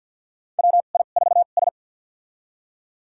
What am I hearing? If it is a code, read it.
Morse code, heard as a keyed tone, about 53 wpm, spaced as for 37 wpm, 700 Hz. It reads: WI4S
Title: USA Call Signs - 4 Characters - 2x Character Spacing - 50wpm